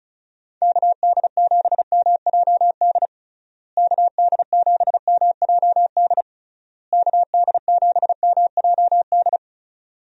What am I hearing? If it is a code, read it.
Morse code, heard as a keyed tone, about 36 wpm, 700 Hz. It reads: KD7MJD KD7MJD KD7MJD
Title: USA Call Signs - 6 Characters - Repeated 3x - 35wpm